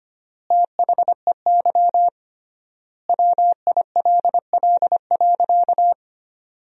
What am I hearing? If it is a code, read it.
Morse code, heard as a keyed tone, about 25 wpm, 700 Hz. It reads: THEY WILL.
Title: 3x Word-Spacing - Sentences from Top 500 Words - 25wpm